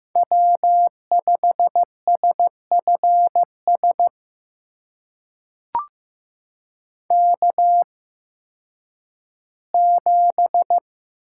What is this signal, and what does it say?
Morse code, heard as a keyed tone, about 15 wpm, spaced as for 10 wpm, 700 Hz. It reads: W5SFS K7
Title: USA Call Signs - 8x Character Spacing - 15wpm